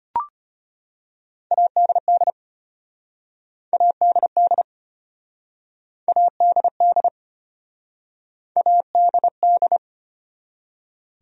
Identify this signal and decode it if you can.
Morse code, heard as a keyed tone, about 33 wpm, 700 Hz. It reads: ADD ADD ADD ADD